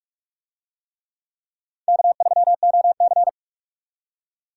Morse code, heard as a keyed tone, about 44 wpm, 700 Hz. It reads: K3OC